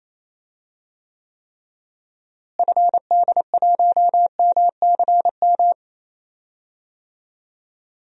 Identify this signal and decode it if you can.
Morse code, heard as a keyed tone, about 28 wpm, 700 Hz. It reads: FD1MCM
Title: DXCC-100 Call Signs - 6 Characters - Repeated 3x - 28wpm